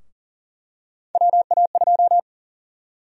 Morse code, heard as a keyed tone, about 40 wpm, 700 Hz. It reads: WA2
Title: USA Call Sign Prefixes - 40wpm